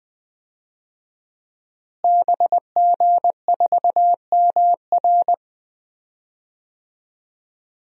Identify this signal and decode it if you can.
Morse code, heard as a keyed tone, about 20 wpm, 700 Hz. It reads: BG4MR
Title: DXCC-100 Call Signs - 5 Characters - Repeated 2x - 20wpm